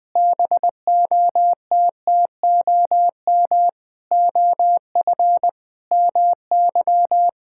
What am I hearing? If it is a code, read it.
Morse code, heard as a keyed tone, about 20 wpm, 700 Hz. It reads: BOTTOM OF MY
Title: Sentences with 6-Word Phrases - Spoken Beginning - 20wpm